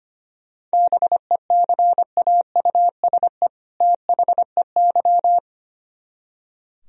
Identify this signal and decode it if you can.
Morse code, heard as a keyed tone, about 33 wpm, 700 Hz. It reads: BECAUSE THEY